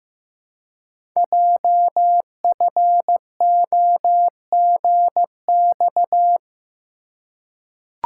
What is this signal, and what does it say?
Morse code, heard as a keyed tone, about 15 wpm, 700 Hz. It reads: JFOGX
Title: Code Groups - 5 Letters - 15wpm